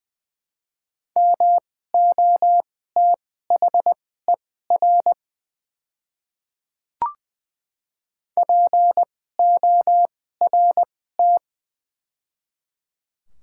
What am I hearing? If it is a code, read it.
Morse code, heard as a keyed tone, about 20 wpm, spaced as for 15 wpm, 700 Hz. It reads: MOTHER PORT
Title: 2x Character Spacing - Top 300 Words - 20wpm